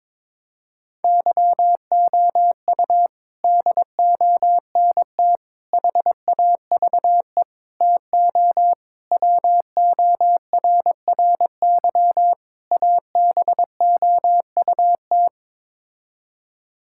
Morse code, heard as a keyed tone, about 22 wpm, 700 Hz. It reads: YOU DONT HAVE TO WORRY ABOUT
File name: practice / 22wpm / Sets of 6 Words - 22wpm